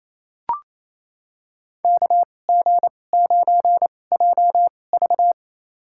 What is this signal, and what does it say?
Morse code, heard as a keyed tone, about 28 wpm, spaced as for 20 wpm, 700 Hz. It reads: KG9JV
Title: USA Call Signs - 2x Character Spacing - 28wpm